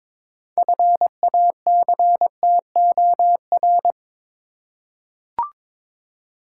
Morse code, heard as a keyed tone, about 22 wpm, 700 Hz. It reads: FACTOR
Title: Six Letter Words - 22wpm